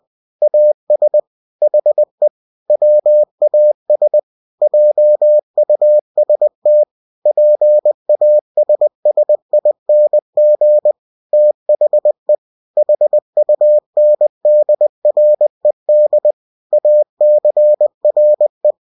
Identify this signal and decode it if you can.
Morse code, heard as a keyed tone, about 20 wpm, 600 Hz. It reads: AS HE WAS JUST PASSING THE HUNDRED ACRE
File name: cw-books / winnie-the-pooh / winnie-20wpm-page-0050